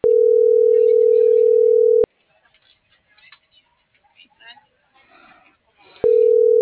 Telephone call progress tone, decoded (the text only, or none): ringback tone